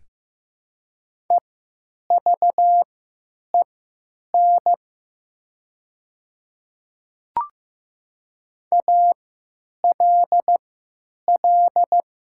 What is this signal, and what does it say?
Morse code, heard as a keyed tone, about 15 wpm, spaced as for 9 wpm, 700 Hz. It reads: EVEN ALL
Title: 3x Character Spacing - Top 200 Words - 15wpm